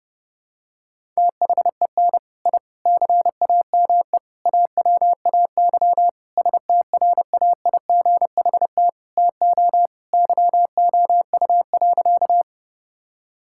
Morse code, heard as a keyed tone, about 30 wpm, 700 Hz. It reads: THEN I CAME AWAY STRAIGHT TO YOU.
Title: Sentences from Top 1000 Words - Review Part 6 - 30wpm